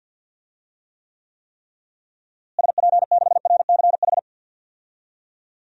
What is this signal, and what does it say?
Morse code, heard as a keyed tone, about 53 wpm, 700 Hz. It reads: SP6RCH
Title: DXCC-100 Call Signs - 6 Characters - Repeated 3x - 50wpm